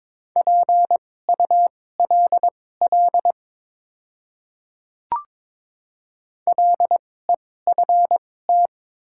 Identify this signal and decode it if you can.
Morse code, heard as a keyed tone, about 22 wpm, spaced as for 16 wpm, 690 Hz. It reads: PULL LEFT
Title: 2x Character Spacing - Top 500 Words - 22wpm